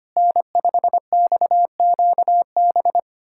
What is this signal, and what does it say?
Morse code, heard as a keyed tone, about 25 wpm, 700 Hz. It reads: N5XQB
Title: USA Callsign DE Callsign - 5 Characters - 25wpm